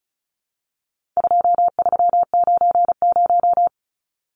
Morse code, heard as a keyed tone, about 34 wpm, 700 Hz. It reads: 2390
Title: Numbers - 4-Digit 35wpm